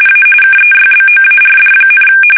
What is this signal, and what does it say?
RTTY, about 45.45 baud. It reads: A KEYBOARD,
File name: fsk850 / teletype-RTTY